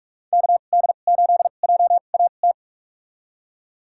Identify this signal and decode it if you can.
Morse code, heard as a keyed tone, about 41 wpm, spaced as for 31 wpm, 700 Hz. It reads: KD8JAT
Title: USA Call Signs - 6 Characters - 2x Character Spacing - 45wpm